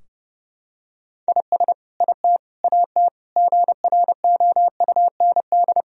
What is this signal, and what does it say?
Morse code, heard as a keyed tone, about 32 wpm, 700 Hz. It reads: IS IT AT GROUND
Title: Sentences from Top 1000 Words - Review Part 4 - 30wpm